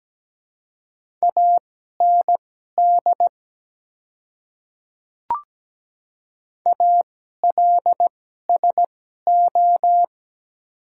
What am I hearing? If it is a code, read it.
Morse code, heard as a keyed tone, about 17 wpm, spaced as for 12 wpm, 700 Hz. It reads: AND ALSO
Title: 2x Character Spacing - Top 200 Words - 17wpm